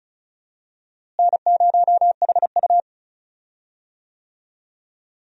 Morse code, heard as a keyed tone, about 36 wpm, 700 Hz. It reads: N0HU
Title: USA Call Signs - 4 Characters - Repeated 3x - 35wpm